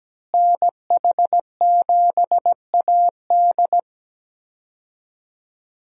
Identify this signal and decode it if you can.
Morse code, heard as a keyed tone, about 17 wpm, 700 Hz. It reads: NH7AD
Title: USA Call Signs - 5 Characters - Repeated 3x - 17wpm